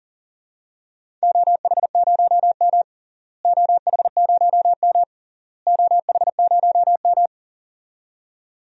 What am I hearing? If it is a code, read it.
Morse code, heard as a keyed tone, about 37 wpm, 700 Hz. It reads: OH0M OH0M OH0M
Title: DXCC-100 Call Signs - 4 Characters - Repeated 3x - 40wpm